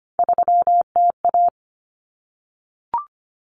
Morse code, heard as a keyed tone, about 25 wpm, 700 Hz. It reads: 3TA
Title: Common Cut Numbers - 3-Digit 25wpm